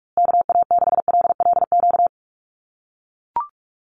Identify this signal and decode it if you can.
Morse code, heard as a keyed tone, about 44 wpm, 700 Hz. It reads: KA6LLQ
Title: USA Call Signs Repeated 3x 45wpm